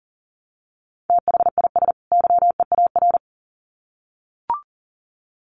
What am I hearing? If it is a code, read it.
Morse code, heard as a keyed tone, about 40 wpm, 700 Hz. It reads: THIS YEAR
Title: Sets of 2 Words 40wpm